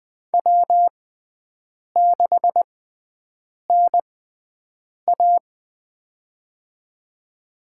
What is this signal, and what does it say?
Morse code, heard as a keyed tone, about 20 wpm, spaced as for 7 wpm, 700 Hz. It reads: W6NA
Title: USA Call Signs - 4 Characters - 6x Character Spacing - 20wpm